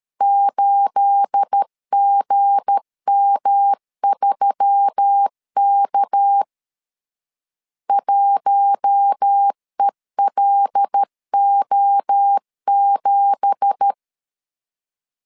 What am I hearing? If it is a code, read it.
Morse code, heard as a keyed tone, about 13 wpm, 800 Hz. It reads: 8GM3K 1ELO7
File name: LEZIONI / 9 2867 12wpm 2322